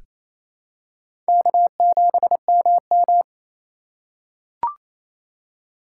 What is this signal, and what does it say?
Morse code, heard as a keyed tone, about 28 wpm, 700 Hz. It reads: K7MM
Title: USA Call Signs - 4 Characters - 2x Character Spacing - 28wpm